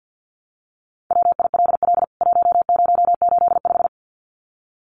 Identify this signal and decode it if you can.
Morse code, heard as a keyed tone, about 50 wpm, 700 Hz. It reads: WILL 1085
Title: CWT Exchange 50wpm